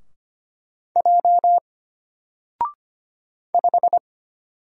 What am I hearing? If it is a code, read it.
Morse code, heard as a keyed tone, about 25 wpm, 700 Hz. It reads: J 5
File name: Single-Letter-Number-25wpm